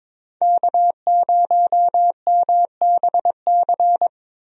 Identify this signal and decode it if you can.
Morse code, heard as a keyed tone, about 22 wpm, 700 Hz. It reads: K0MBC
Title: USA Call Signs - 5 Characters - Repeated 3x - 22wpm